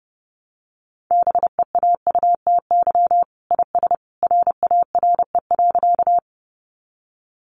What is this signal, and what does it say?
Morse code, heard as a keyed tone, about 30 wpm, 700 Hz. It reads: BEAUTY IS RARE.
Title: Sentences from Top 1100 Words 30wpm